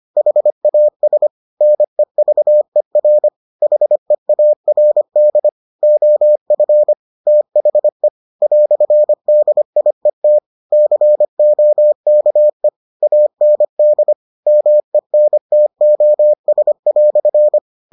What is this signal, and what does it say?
Morse code, heard as a keyed tone, about 25 wpm, 600 Hz. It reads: HAS NEVER HEARD OF THE "DIET COKE AND MENTOS"